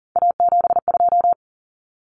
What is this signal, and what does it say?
Morse code, heard as a keyed tone, about 40 wpm, 700 Hz. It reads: A72
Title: Common Cut Numbers - 3-Digit 40wpm